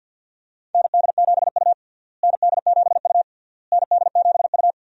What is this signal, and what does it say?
Morse code, heard as a keyed tone, about 47 wpm, 700 Hz. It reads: ND7U ND7U ND7U
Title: USA Call Signs - 4 Characters - Repeated 3x - 50wpm